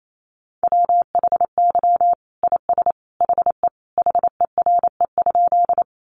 Morse code, heard as a keyed tone, about 28 wpm, 700 Hz. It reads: WHY IS HE HERE?
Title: Sentences from Top 200 Words 28wpm